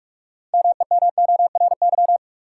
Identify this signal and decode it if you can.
Morse code, heard as a keyed tone, about 46 wpm, 700 Hz. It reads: MEMORY
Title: Sentences from Top 1300 Words - New Word Drill - 45wpm